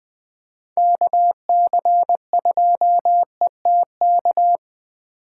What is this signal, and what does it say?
Morse code, heard as a keyed tone, about 21 wpm, 700 Hz. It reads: KC2ETK
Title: Speed Racing - USA Call Signs 17wpm